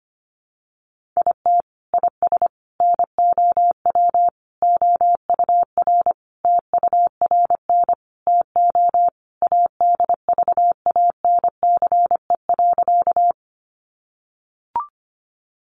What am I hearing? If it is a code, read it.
Morse code, heard as a keyed tone, about 25 wpm, 700 Hz. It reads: IT IS NOW OUR TURN TO ADVANCE.
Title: Sentences from Top 1400 Words 25wpm